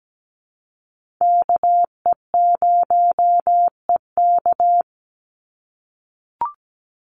Morse code, heard as a keyed tone, about 17 wpm, 700 Hz. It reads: KE0EK
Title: Call Signs - 17wpm